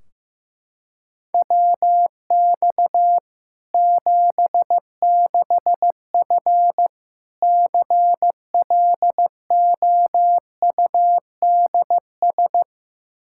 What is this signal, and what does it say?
Morse code, heard as a keyed tone, about 15 wpm, 700 Hz. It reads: WX 76F CLOUDS
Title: Practice QSO - Exchange 2 - Info - 15wpm